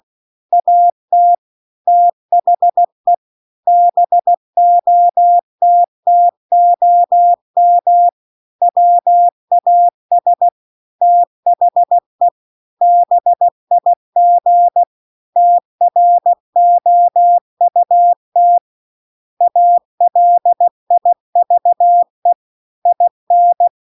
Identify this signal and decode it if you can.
Morse code, heard as a keyed tone, about 16 wpm, 700 Hz. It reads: AT THE BOTTOM WAS THE BIG TROUT ALIVE IN
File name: CW-The-Big-Twoheaded-River-2-16WPM